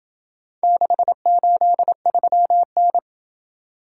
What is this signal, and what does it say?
Morse code, heard as a keyed tone, about 27 wpm, 700 Hz. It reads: 683N